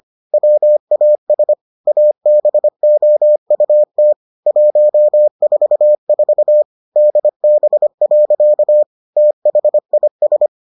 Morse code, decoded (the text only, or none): WAS ABOUT 144 DB. THIS